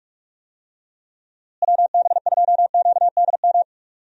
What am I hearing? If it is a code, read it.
Morse code, heard as a keyed tone, about 44 wpm, 700 Hz. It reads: WB2QDM
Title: USA Call Signs - 6 Characters - Repeated 3x - 45wpm